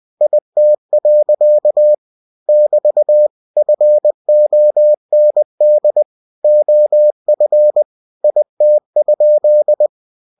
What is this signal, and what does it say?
Morse code, heard as a keyed tone, about 20 wpm, 600 Hz. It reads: IT. <BT> FOND OF IT?
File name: cw-books / winnie-the-pooh / winnie-20wpm-page-0032